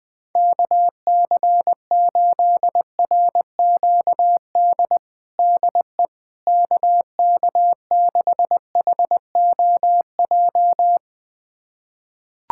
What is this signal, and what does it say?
Morse code, heard as a keyed tone, about 20 wpm, 700 Hz. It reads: KC8RQD DE KK6HOJ
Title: USA Callsign DE Callsign - 6 Characters - 20wpm